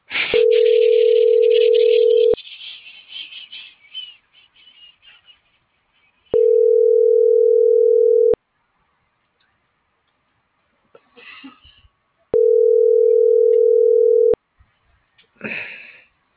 Ringback tone (North America), 440 + 480 Hz.